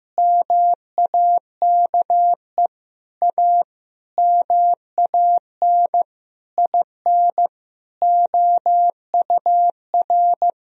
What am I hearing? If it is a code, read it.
Morse code, heard as a keyed tone, about 15 wpm, 700 Hz. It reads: MAKE A MAN IN OUR
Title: Sentences from Top 1000 Words - Review Part 4 - 15wpm